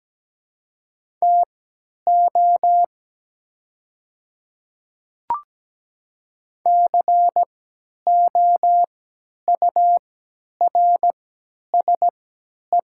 Morse code, heard as a keyed tone, about 17 wpm, spaced as for 10 wpm, 700 Hz. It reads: TO COURSE
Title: 3x Character Spacing - Top 500 Words - 17wpm